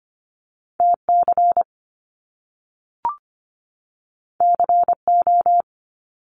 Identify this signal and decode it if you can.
Morse code, heard as a keyed tone, about 25 wpm, 700 Hz. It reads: TC CO
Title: Top Two Letter Combinations in Words 25wpm